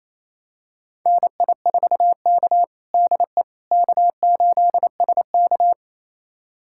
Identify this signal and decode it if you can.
Morse code, heard as a keyed tone, about 28 wpm, 690 Hz. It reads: NI4K DE K8SK
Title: USA Callsign DE Callsign - 4 Characters - 28wpm